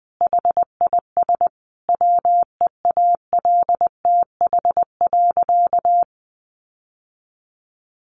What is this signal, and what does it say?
Morse code, heard as a keyed tone, about 20 wpm, 700 Hz. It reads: HIS WEALTH.
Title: Sentences from Top 1200 Words 20wpm